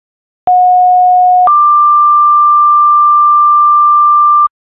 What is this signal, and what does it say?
Two-tone page, heard as a one-second tone, then a three-second tone, about 730 then 1180 Hz. It is heard from 0.5 to 4.5 s.